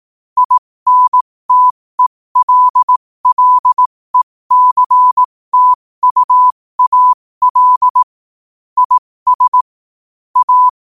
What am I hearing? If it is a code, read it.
Morse code, heard as a keyed tone, about 18 wpm, spaced as for 15 wpm, 1000 Hz. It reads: INTELLECTUAL IS A
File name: qotdfiles / 15wpm